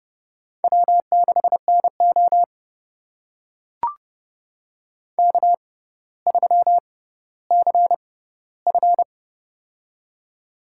Morse code, heard as a keyed tone, about 32 wpm, spaced as for 16 wpm, 700 Hz. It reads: W6NO K3CF
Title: USA Call Signs - 6x Character Spacing - 30wpm